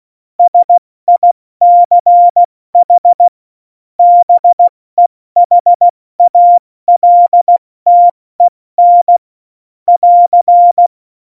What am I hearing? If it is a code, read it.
Morse code, heard as a keyed tone, about 16 wpm, 700 Hz. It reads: SICH BEHALTEN <AR>